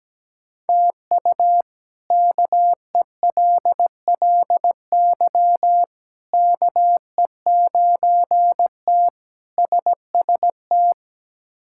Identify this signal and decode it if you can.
Morse code, heard as a keyed tone, about 17 wpm, 700 Hz. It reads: TU KELLY KE9T SST
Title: K1USN SST QSO Practice - 17wpm